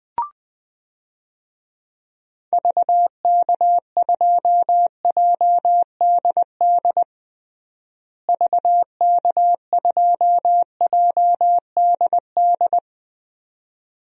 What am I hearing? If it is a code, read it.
Morse code, heard as a keyed tone, about 20 wpm, 700 Hz. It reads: VK2JDD VK2JDD
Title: DXCC-100 Call Signs - 6 Characters - Repeated 2x - 20wpm